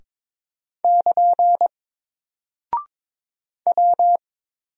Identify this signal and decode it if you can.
Morse code, heard as a keyed tone, about 22 wpm, 700 Hz. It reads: <KN> W
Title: Single Character (Letter:Number:Punctuation:Prosign) Alt 22wpm